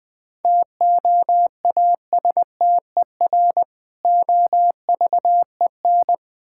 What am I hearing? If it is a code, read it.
Morse code, heard as a keyed tone, about 21 wpm, 700 Hz. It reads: TOASTER OVEN
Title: Difficult Words - Speed-Racing with Hints - 20wpm